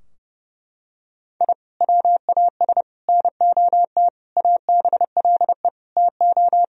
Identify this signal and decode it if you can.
Morse code, heard as a keyed tone, about 30 wpm, 700 Hz. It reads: I WAS NOT ABLE TO
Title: Sentences from Top 1000 Words - Review Part 4 - 30wpm